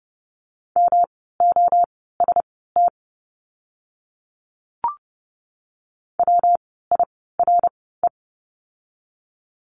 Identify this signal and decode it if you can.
Morse code, heard as a keyed tone, about 30 wpm, spaced as for 17 wpm, 700 Hz. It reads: MOST WIRE
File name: course / 30:10wpm / 115 Lesson 19f - Letter w - Words 30wpm